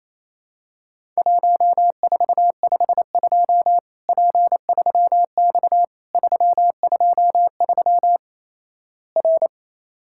Morse code, heard as a keyed tone, about 28 wpm, 700 Hz. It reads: 1452 P3X 323 R
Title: Worked All Europe DX Contest - QTC Exchange - 28wpm